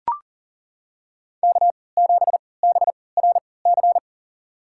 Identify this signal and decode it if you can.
Morse code, heard as a keyed tone, about 40 wpm, spaced as for 23 wpm, 700 Hz. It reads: K7BRC
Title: USA Call Signs - 3x Character Spacing - 40wpm